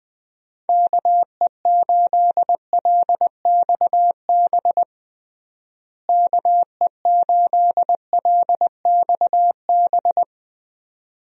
Morse code, heard as a keyed tone, about 21 wpm, 700 Hz. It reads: KE8LXB KE8LXB
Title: USA Call Signs - 6 Characters - Repeated 2x - 20wpm